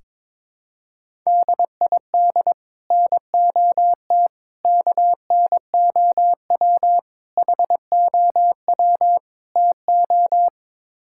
Morse code, heard as a keyed tone, about 22 wpm, 700 Hz. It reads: DID NOT KNOW HOW TO
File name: practice / 22wpm / Build A Phrase - 5-Word Phrases - 22wpm